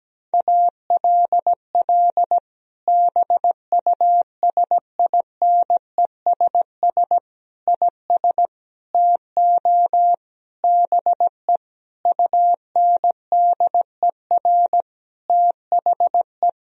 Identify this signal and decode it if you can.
Morse code, heard as a keyed tone, about 17 wpm, 700 Hz. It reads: ALL BUSINESS IS TO BE UNDER THE
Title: Sentences from Top 1000 Words - Review Part 3 - 17wpm